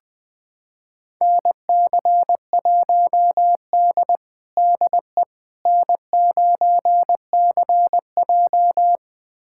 Morse code, heard as a keyed tone, about 21 wpm, 700 Hz. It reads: NC1D DE N9CJ
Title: USA Callsign DE Callsign - 4 Characters - 20wpm